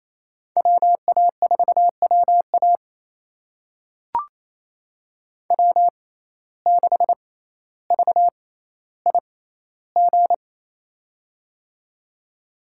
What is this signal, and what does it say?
Morse code, heard as a keyed tone, about 28 wpm, spaced as for 14 wpm, 700 Hz. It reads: WA4WA W6VIG
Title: USA Call Signs - 5 Characters - 6x Character Spacing - 28wpm